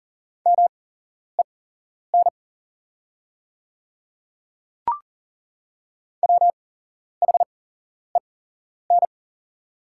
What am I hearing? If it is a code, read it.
Morse code, heard as a keyed tone, about 40 wpm, spaced as for 11 wpm, 700 Hz. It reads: MEN WHEN
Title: 8x Character Spacing - Top 300 Words - 40wpm